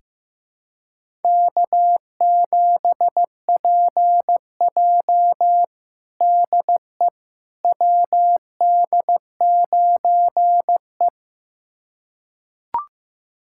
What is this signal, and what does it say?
Morse code, heard as a keyed tone, about 15 wpm, 700 Hz. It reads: K7PJ DE WD9E
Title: USA Callsign DE Callsign - 4 Characters - 15wpm